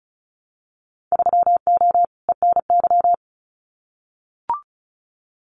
Morse code, heard as a keyed tone, about 34 wpm, 700 Hz. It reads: 3O ENY
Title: Winter Field Day Exchanges Encore 35wpm